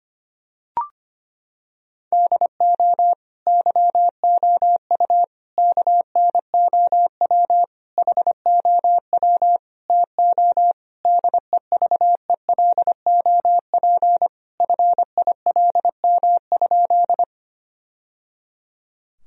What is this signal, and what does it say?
Morse code, heard as a keyed tone, about 25 wpm, 700 Hz. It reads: DO YOU KNOW HOW TO DEVELOP FILM?
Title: Sentences from Top 1000 Words - Review Part 3 - 25wpm